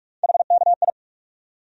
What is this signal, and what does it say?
Morse code, heard as a keyed tone, about 46 wpm, 700 Hz. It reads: HKI
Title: USA Call Sign Postfixes - 3 Letters 45wpm